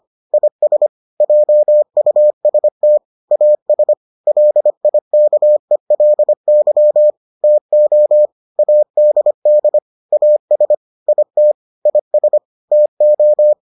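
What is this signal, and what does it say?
Morse code, heard as a keyed tone, about 25 wpm, 600 Hz. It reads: IS JUST AS LIKELY TO ADD AS IT IS TO
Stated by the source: Morse edition of the Foundations of Amateur Radio episodes.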